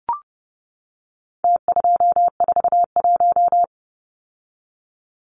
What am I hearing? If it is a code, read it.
Morse code, heard as a keyed tone, about 30 wpm, 700 Hz. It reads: T241
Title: Numbers - Contest Style - 4-Digit 30wpm